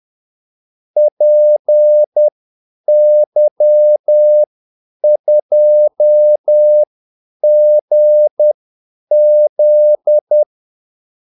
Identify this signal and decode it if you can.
Morse code, heard as a keyed tone, about 10 wpm, spaced as for 8 wpm, 600 Hz. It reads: PY2GZ
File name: Callsigns-10-8wpm